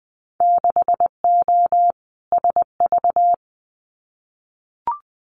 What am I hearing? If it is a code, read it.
Morse code, heard as a keyed tone, about 21 wpm, 700 Hz. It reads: 6O SV
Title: Winter Field Day Exchanges Encore 20wpm